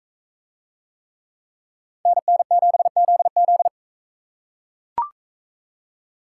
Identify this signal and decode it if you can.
Morse code, heard as a keyed tone, about 39 wpm, 700 Hz. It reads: NN7ZZ